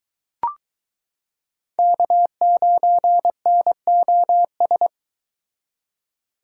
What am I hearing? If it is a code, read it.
Morse code, heard as a keyed tone, about 22 wpm, 700 Hz. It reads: K9NOS